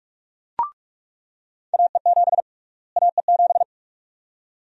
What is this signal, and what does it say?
Morse code, heard as a keyed tone, about 47 wpm, 700 Hz. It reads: AE7 AE7
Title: USA Call Sign Prefixes - Repeated 2x - 45wpm